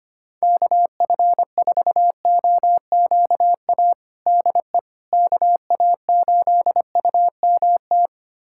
Morse code, heard as a keyed tone, about 25 wpm, 700 Hz. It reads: KF4OQA DE KA8UMT
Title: USA Callsign DE Callsign - 6 Characters - 25wpm